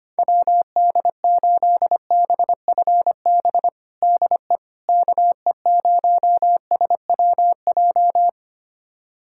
Morse code, heard as a keyed tone, about 25 wpm, 700 Hz. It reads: WD8BFB DE KE0SWJ
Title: USA Callsign DE Callsign - 6 Characters - 25wpm